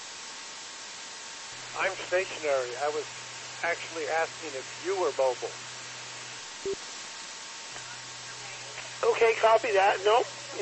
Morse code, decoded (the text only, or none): none